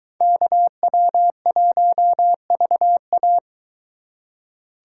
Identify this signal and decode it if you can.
Morse code, heard as a keyed tone, about 22 wpm, 700 Hz. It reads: KW1VA